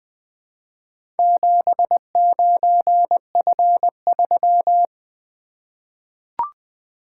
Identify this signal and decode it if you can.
Morse code, heard as a keyed tone, about 20 wpm, 700 Hz. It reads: 79F3
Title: Four Letter-Number Combinations - 20wpm